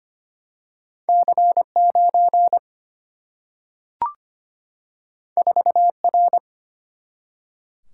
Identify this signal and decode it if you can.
Morse code, heard as a keyed tone, about 25 wpm, 700 Hz. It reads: C9 4R
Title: Single-Letter Single-Number 25wpm